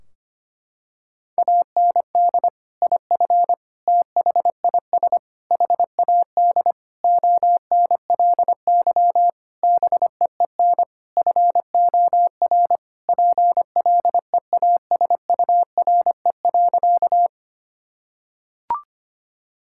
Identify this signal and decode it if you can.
Morse code, heard as a keyed tone, about 25 wpm, 700 Hz. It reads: AND IF THIS HAD ONLY BEEN FOR PLEASURE.
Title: Sentences from Top 1600 Words 25wpm